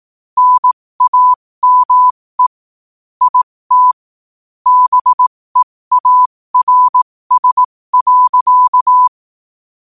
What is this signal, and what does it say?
Morse code, heard as a keyed tone, about 18 wpm, spaced as for 15 wpm, 1000 Hz. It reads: NAME IT BEARS.
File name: qotdfiles / 15wpm